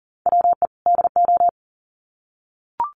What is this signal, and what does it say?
Morse code, heard as a keyed tone, about 40 wpm, 700 Hz. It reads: WE DO
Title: Sets of 2 Words 40wpm